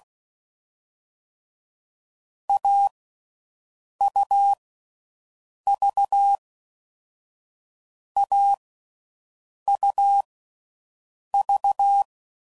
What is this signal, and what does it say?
Morse code, heard as a keyed tone, about 16 wpm, 800 Hz. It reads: A U V A U V
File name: Combinations / Difficult Combos / AUV AUV4